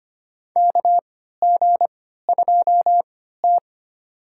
Morse code, heard as a keyed tone, about 25 wpm, spaced as for 14 wpm, 700 Hz. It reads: KG2T
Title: USA Call Signs - 4 Characters - 3x Character Spacing - 25wpm